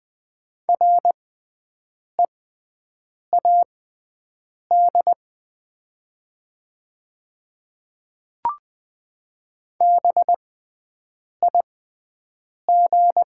Morse code, heard as a keyed tone, about 20 wpm, spaced as for 7 wpm, 700 Hz. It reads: READ BIG
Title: 6x Character Spacing - Top 200 Words - 20wpm